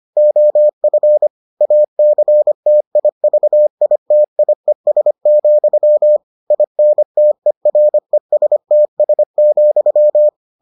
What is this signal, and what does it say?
Morse code, heard as a keyed tone, about 25 wpm, 600 Hz. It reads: OF ACTIVITIES, INTERESTS,